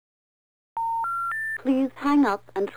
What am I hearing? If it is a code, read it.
Special information tone, with speech.